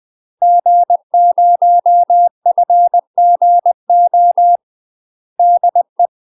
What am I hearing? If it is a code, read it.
Morse code, heard as a keyed tone, about 20 wpm, 700 Hz. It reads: G0FGO DE
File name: UK MASTER QSOs TRAINING / 20WPM / UK MASTER QSOs 0022